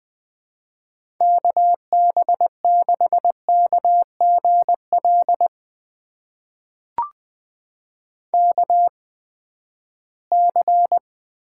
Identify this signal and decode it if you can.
Morse code, heard as a keyed tone, about 20 wpm, spaced as for 14 wpm, 700 Hz. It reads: KB6KGL KC
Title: USA Call Signs - 6 Characters - 8x Character Spacing - 20wpm